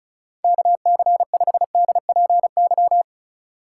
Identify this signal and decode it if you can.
Morse code, heard as a keyed tone, about 36 wpm, 700 Hz. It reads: KC5DPY
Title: Speed Racing - USA 2x3 Call Signs 30wpm